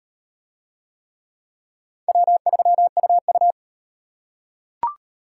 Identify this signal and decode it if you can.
Morse code, heard as a keyed tone, about 36 wpm, 700 Hz. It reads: W3UU